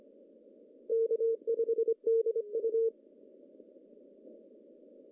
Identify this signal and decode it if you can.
Morse code, heard as a keyed tone, about 25 wpm, 450 Hz. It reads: K5DU
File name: CW / cqww-cw-k5tr